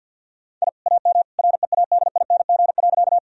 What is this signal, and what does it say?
Morse code, heard as a keyed tone, about 50 wpm, 700 Hz. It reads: I AM READING.
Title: Sentences from Top 1000 Words - Review Part 3 - 50wpm